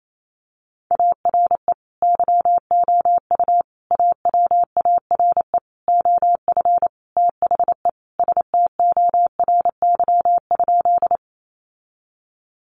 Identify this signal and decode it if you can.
Morse code, heard as a keyed tone, about 28 wpm, 700 Hz. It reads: ARE YOU AWARE OF THE STORY?
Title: Sentences from Top 1200 Words 28wpm